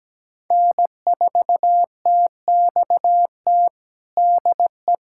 Morse code, heard as a keyed tone, about 17 wpm, 700 Hz. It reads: N4TXT DE
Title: USA Callsign DE Callsign - 5 Characters - 17wpm